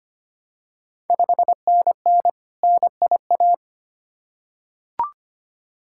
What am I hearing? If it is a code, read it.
Morse code, heard as a keyed tone, about 25 wpm, 700 Hz. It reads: 5NN NIA
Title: Ontario QSO Party Exchange 25wpm